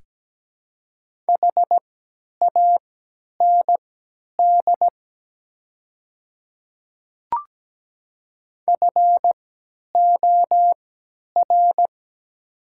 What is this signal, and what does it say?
Morse code, heard as a keyed tone, about 17 wpm, spaced as for 10 wpm, 700 Hz. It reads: HAND FOR
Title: 3x Character Spacing - Top 200 Words - 17wpm